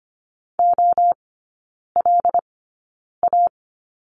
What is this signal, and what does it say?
Morse code, heard as a keyed tone, about 25 wpm, spaced as for 9 wpm, 700 Hz. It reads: OLA
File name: Lesson 17e - Letter u - Three Characters 25wpm